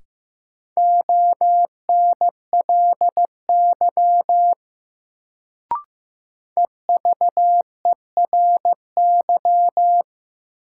Morse code, heard as a keyed tone, about 15 wpm, 700 Hz. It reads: ONLY EVERY
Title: Sentences from Top 200 Words - New Word Drill - 15wpm